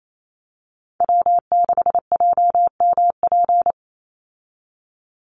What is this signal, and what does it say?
Morse code, heard as a keyed tone, about 28 wpm, 700 Hz. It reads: W6JMP
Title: Call Signs - 28wpm